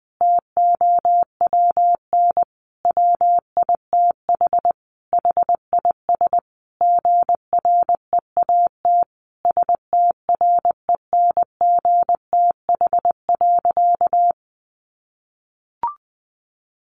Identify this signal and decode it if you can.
Morse code, heard as a keyed tone, about 19 wpm, 700 Hz. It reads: TOWN WITH HIS GREAT STRENGTH.
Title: Sentences from Top 1300 Words 20wpm